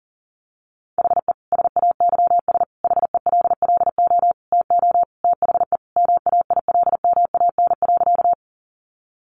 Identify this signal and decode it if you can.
Morse code, heard as a keyed tone, about 40 wpm, 700 Hz. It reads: HE SAYS HELLO TO THE MAILMAN.
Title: Easy Phrases 40wpm